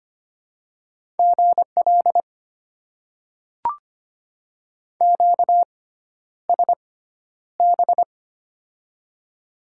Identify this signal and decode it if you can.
Morse code, heard as a keyed tone, about 25 wpm, spaced as for 11 wpm, 700 Hz. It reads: GL QSB
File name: Top 90 QSO Elements - 6x Character Spacing - 25wpm